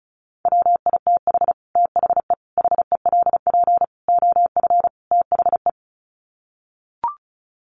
Morse code, heard as a keyed tone, about 34 wpm, 700 Hz. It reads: WITH THE HELP OF THE
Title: Sets of 5 Words Encore 35wpm